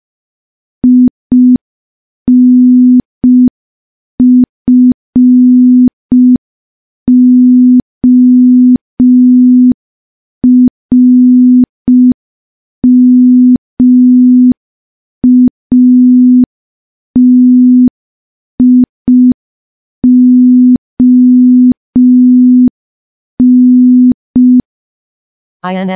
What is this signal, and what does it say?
Morse code, heard as a keyed tone, about 5 wpm, 250 Hz. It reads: INFORMATION
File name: morse / morse-training-05WPM